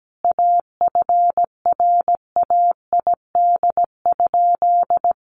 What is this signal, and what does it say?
Morse code, heard as a keyed tone, about 17 wpm, 700 Hz. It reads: AFRAID?
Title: Sentences from Top 1500 Words 17wpm